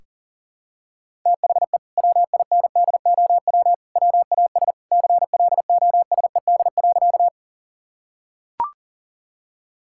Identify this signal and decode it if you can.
Morse code, heard as a keyed tone, about 40 wpm, 700 Hz. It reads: THE WINDOW WAS CLOSED.
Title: Sentences from Top 1600 Words 40wpm